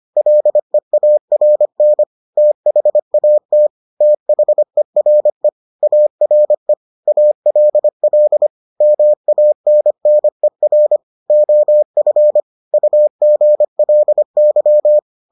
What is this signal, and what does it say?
Morse code, heard as a keyed tone, about 25 wpm, 600 Hz. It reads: LEARN THAT THERE ARE ALL MANNER OF UGLY